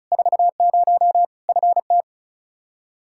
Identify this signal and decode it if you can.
Morse code, heard as a keyed tone, about 34 wpm, 700 Hz. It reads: 40 FT
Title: K1USN SST QSO Practice 35wpm